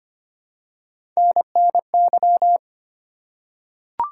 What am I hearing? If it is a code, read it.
Morse code, heard as a keyed tone, about 25 wpm, 700 Hz. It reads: NNY
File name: practice / 25wpm / ARRL Sections v2 - 25wpm